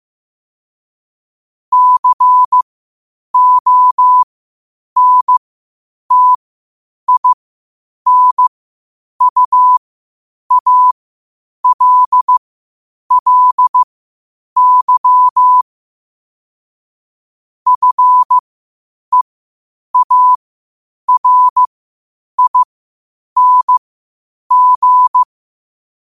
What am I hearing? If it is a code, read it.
Morse code, heard as a keyed tone, about 15 wpm, spaced as for 8 wpm, 1000 Hz. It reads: CONTINUALLY FEARING
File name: qotdfiles / 8wpm